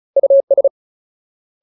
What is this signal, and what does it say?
Morse code, heard as a keyed tone, about 34 wpm, 550 Hz. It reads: US
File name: Ultimate Word List / 35WPM Ultimate